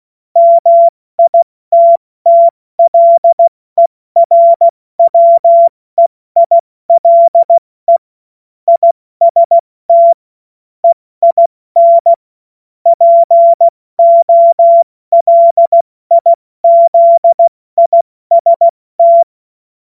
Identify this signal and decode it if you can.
Morse code, heard as a keyed tone, about 16 wpm, 680 Hz. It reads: MITTLERWEILE IST EIN POLIZIST
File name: Morsekurs / Witze / Witz 02. August 2022 70 F80